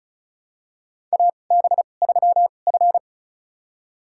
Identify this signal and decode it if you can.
Morse code, heard as a keyed tone, about 36 wpm, spaced as for 25 wpm, 700 Hz. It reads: AB3F
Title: USA Call Signs - 4 Characters - 2x Character Spacing - 35wpm